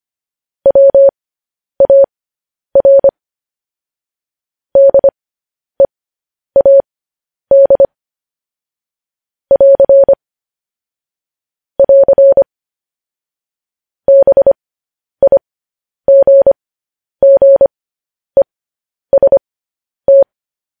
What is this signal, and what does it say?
Morse code, heard as a keyed tone, about 25 wpm, spaced as for 10 wpm, 550 Hz. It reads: WAR DEAD <AR> <AR> BIGGEST
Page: News Headlines in Morse Code at 10 WPM